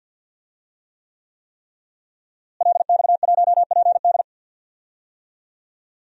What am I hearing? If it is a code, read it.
Morse code, heard as a keyed tone, about 47 wpm, 700 Hz. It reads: LX1PD
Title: DXCC-100 Call Signs - 5 Characters - Repeated 3x - 50wpm